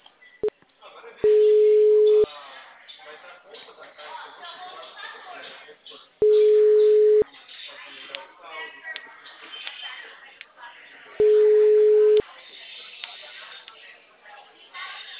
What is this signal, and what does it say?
Ringback tone (Europe, 425 Hz), with speech.